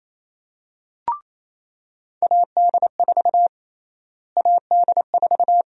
Morse code, heard as a keyed tone, about 28 wpm, 700 Hz. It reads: AD4 AD4
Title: USA Call Sign Prefixes - Repeated 2x - 28wpm